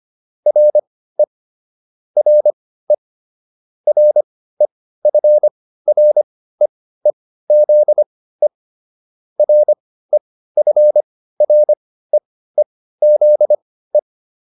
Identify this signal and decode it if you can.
Morse code, heard as a keyed tone, about 25 wpm, spaced as for 15 wpm, 600 Hz. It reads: RE RE REFREEZE REFREEZE